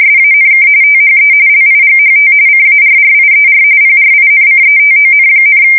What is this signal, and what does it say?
RTTY, about 45.45 baud. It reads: SANK THE LIGHT CARRIER PRINCETON,
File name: fsk170 / USS-New-Jersey